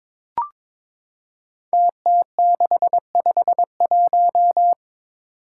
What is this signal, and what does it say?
Morse code, heard as a keyed tone, about 22 wpm, 700 Hz. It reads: TT651
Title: Five Challenging Letter-Number Combinations 22wpm